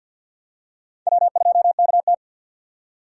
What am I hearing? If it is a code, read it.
Morse code, heard as a keyed tone, about 47 wpm, 700 Hz. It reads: W2KT
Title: USA Call Signs - 4 Characters - 6x Character Spacing - 50wpm